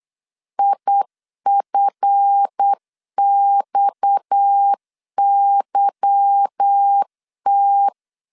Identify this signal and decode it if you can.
Morse code, heard as a keyed tone, about 8 wpm, 800 Hz. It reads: IFXYT
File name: LEZIONI / 8 YJ45 / 8 YJ45 8wpm 3214